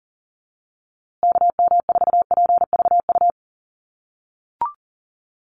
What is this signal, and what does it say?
Morse code, heard as a keyed tone, about 38 wpm, 700 Hz. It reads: KM4PVU
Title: USA Call Signs Repeated 3x 40wpm